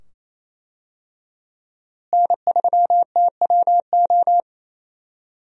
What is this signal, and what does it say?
Morse code, heard as a keyed tone, about 28 wpm, 700 Hz. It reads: N3TWO
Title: USA Call Signs - 5 Characters - Repeated 2x - 28wpm